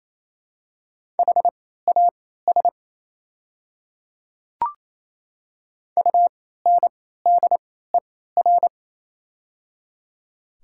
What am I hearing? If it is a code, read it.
Morse code, heard as a keyed tone, about 28 wpm, spaced as for 16 wpm, 700 Hz. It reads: HAS UNDER
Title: 3x Character Spacing - Top 200 Words - 28wpm